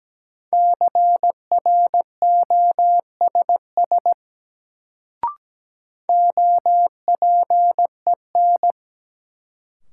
Morse code, heard as a keyed tone, about 17 wpm, 700 Hz. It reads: CROSS OPEN
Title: Sentences from Top 300 Words - New Word Drill - 17wpm